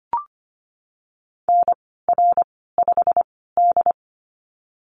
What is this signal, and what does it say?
Morse code, heard as a keyed tone, about 25 wpm, spaced as for 16 wpm, 700 Hz. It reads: NR5D
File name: Lesson 15g - Number 5 - Call Signs 25wpm